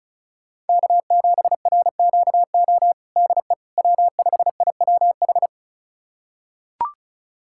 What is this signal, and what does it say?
Morse code, heard as a keyed tone, about 36 wpm, 700 Hz. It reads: K7RQO DE W5IWH
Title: USA Callsign DE Callsign - 5 Characters - 35wpm